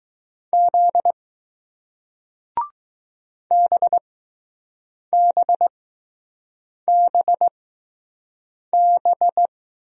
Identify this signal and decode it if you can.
Morse code, heard as a keyed tone, about 20 wpm, spaced as for 6 wpm, 700 Hz. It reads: Z BBBB